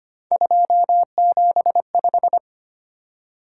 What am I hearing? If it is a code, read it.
Morse code, heard as a keyed tone, about 25 wpm, 700 Hz. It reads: 275
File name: Three Letter-Number Combinations - 25wpm